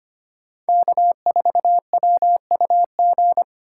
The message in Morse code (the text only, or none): K4WUG